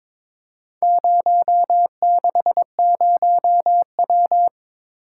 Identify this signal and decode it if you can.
Morse code, heard as a keyed tone, about 23 wpm, 700 Hz. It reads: 060W
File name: practice / 22wpm / Four Letter-Number Combinations - 22wpm